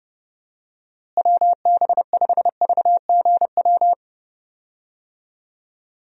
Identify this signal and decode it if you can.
Morse code, heard as a keyed tone, about 31 wpm, 700 Hz. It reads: WB5VGW
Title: USA Call Signs - 6 Characters - Repeated 3x - 30wpm